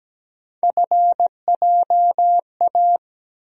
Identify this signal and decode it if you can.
Morse code, heard as a keyed tone, about 17 wpm, 700 Hz. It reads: FJA
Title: USA Call Sign Postfixes - 3 Letters 17wpm